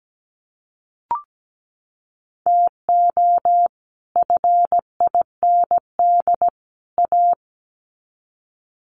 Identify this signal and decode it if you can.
Morse code, heard as a keyed tone, about 17 wpm, 700 Hz. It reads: TO FIND A
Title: Sets of 3 Words 17wpm